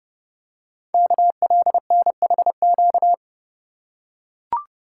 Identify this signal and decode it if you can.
Morse code, heard as a keyed tone, about 30 wpm, 700 Hz. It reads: KLNHQ
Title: Code Groups - 5 Letters - 30wpm